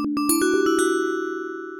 There is no code.